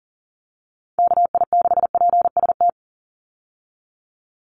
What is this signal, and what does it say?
Morse code, heard as a keyed tone, about 40 wpm, 700 Hz. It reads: KI6PST
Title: USA Call Signs 40wpm